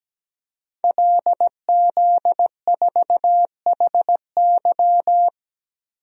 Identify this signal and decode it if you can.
Morse code, heard as a keyed tone, about 17 wpm, 700 Hz. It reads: LZ4HY
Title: DXCC-100 Call Signs - 5 Characters - Repeated 3x - 17wpm